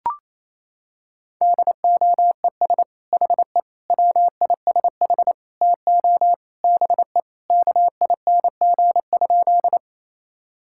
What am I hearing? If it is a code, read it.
Morse code, heard as a keyed tone, about 28 wpm, 700 Hz. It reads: DOES HE WISH TO BE KING?
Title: Sentences from Top 1600 Words 28wpm